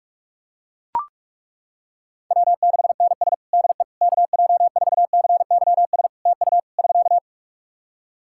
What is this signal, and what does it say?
Morse code, heard as a keyed tone, about 41 wpm, 700 Hz. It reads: W6NS DE KJ4CYS TU <SK>
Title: Practice QSO - Exchange 4 - Ending - 45wpm